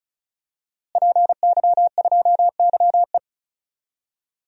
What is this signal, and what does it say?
Morse code, heard as a keyed tone, about 36 wpm, 700 Hz. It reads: PY2YE
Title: DXCC-100 Call Signs - 5 Characters - Repeated 2x - 35wpm